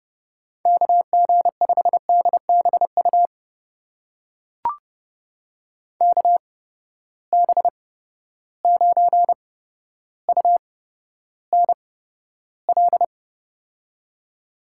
Morse code, heard as a keyed tone, about 30 wpm, spaced as for 13 wpm, 700 Hz. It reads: KG5DBU KB9UNL
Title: USA Call Signs - 6 Characters - 8x Character Spacing - 30wpm